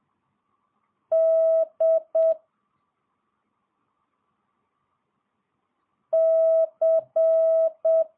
Morse code, heard as a keyed tone, about 7 wpm, 650 Hz. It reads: D C
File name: jh8cbh / 2022cw